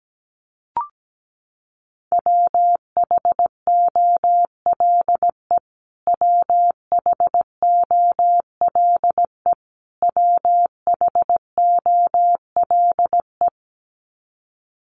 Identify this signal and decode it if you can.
Morse code, heard as a keyed tone, about 17 wpm, 700 Hz. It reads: WHOLE WHOLE WHOLE
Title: Top 500 Words Repeated 17wpm